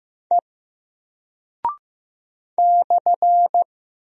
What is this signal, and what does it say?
Morse code, heard as a keyed tone, about 15 wpm, 700 Hz. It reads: E /
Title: Single Character (Letter:Number:Punctuation:Prosign) 15wpm